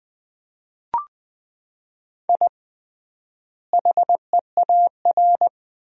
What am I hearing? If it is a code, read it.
Morse code, heard as a keyed tone, about 20 wpm, 700 Hz. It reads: I HEAR
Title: 3x Word-Spacing - Sentences from Top 500 Words - 20wpm